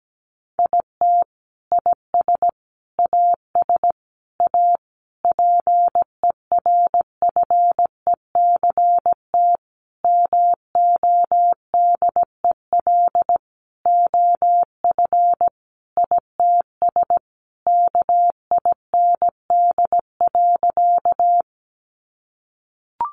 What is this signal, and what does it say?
Morse code, heard as a keyed tone, about 17 wpm, 700 Hz. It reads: IT IS AS A PERFECT MODEL OF ITS KIND.
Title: Sentences from Top 1000 Words 17wpm